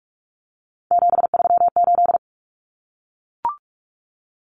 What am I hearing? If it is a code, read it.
Morse code, heard as a keyed tone, about 44 wpm, 700 Hz. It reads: 738
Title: Common Cut Numbers - 3-Digit 45wpm